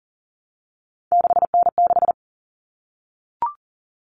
Morse code, heard as a keyed tone, about 40 wpm, 700 Hz. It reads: 6N6
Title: All Cut Numbers - 3-Digit 40wpm